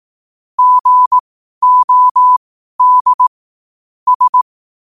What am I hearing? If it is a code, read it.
Morse code, heard as a keyed tone, about 18 wpm, spaced as for 12 wpm, 1000 Hz. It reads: GOD S
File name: qotdfiles / 12wpm